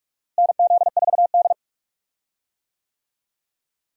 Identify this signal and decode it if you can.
Morse code, heard as a keyed tone, about 42 wpm, 700 Hz. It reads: NZ4D
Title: USA Call Signs - 4 Characters - Repeated 2x - 45wpm